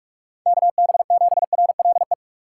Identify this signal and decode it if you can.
Morse code, heard as a keyed tone, about 44 wpm, 700 Hz. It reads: KB7RLE
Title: USA Call Signs - 6 Characters - 6x Character Spacing - 45wpm